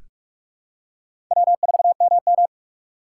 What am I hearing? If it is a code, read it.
Morse code, heard as a keyed tone, about 46 wpm, 700 Hz. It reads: W4MM
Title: USA Call Signs - 4 Characters - 3x Character Spacing - 45wpm